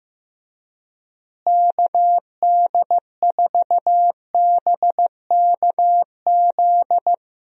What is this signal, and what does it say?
Morse code, heard as a keyed tone, about 15 wpm, 700 Hz. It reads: KD4BKZ
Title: USA Call Signs - 6 Characters - Repeated 3x - 15wpm